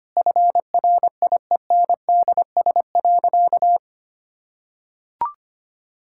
Morse code, heard as a keyed tone, about 25 wpm, 700 Hz. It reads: FRIENDS.
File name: practice / 25wpm / Sentences with 2-Word Phrases - 25wpm